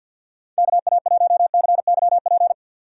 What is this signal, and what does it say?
Morse code, heard as a keyed tone, about 50 wpm, 700 Hz. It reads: KA1CYP
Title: USA Call Signs - 6 Characters - 50wpm